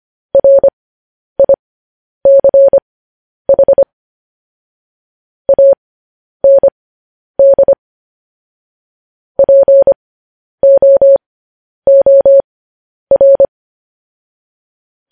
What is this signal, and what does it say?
Morse code, heard as a keyed tone, about 25 wpm, spaced as for 10 wpm, 550 Hz. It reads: RICH AND POOR
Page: news headlines in morse code at 10 wpm